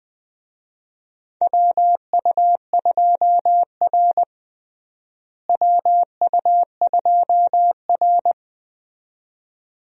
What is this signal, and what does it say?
Morse code, heard as a keyed tone, about 20 wpm, 700 Hz. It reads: WU2R WU2R
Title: USA Call Signs - 4 Characters - Repeated 2x - 20wpm